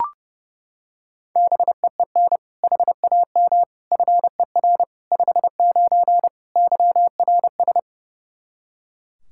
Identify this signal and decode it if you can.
Morse code, heard as a keyed tone, about 30 wpm, 700 Hz. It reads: BEEN HAM FER 59 YRS
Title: Practice QSO - Exchange 3 - More Info - 30wpm